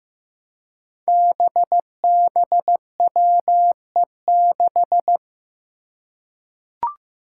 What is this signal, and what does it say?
Morse code, heard as a keyed tone, about 15 wpm, 700 Hz. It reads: BBWE6
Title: Five Challenging Letter-Number Combinations 15wpm